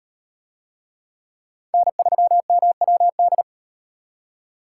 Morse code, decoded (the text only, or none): N3MWD